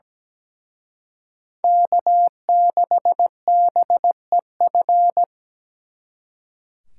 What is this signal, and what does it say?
Morse code, heard as a keyed tone, about 20 wpm, 700 Hz. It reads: K6BEF
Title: Speed Racing - USA 1x3 Call Signs 17wpm